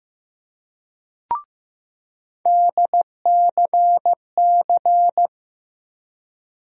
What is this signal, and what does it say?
Morse code, heard as a keyed tone, about 15 wpm, 700 Hz. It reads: DCC
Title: USA Call Sign Postfixes - 3 Letters 15wpm